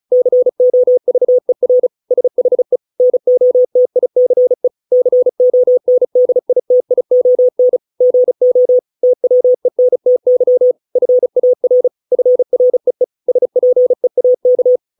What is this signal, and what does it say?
Morse code, heard as a keyed tone, about 34 wpm, 500 Hz. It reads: COVER SHE NOTICE CONDITION GO TWENTY FAR FREE SPEAK